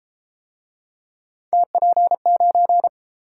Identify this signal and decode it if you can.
Morse code, heard as a keyed tone, about 29 wpm, 700 Hz. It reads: TP9